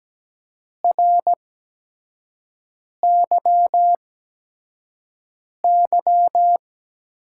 Morse code, heard as a keyed tone, about 17 wpm, spaced as for 5 wpm, 700 Hz. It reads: RYY
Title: USA Call Signs - 5 Characters - 8x Character Spacing - 17wpm